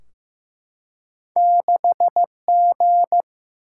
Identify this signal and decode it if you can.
Morse code, heard as a keyed tone, about 15 wpm, 700 Hz. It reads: 6G
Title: Single-Letter Single-Number 15wpm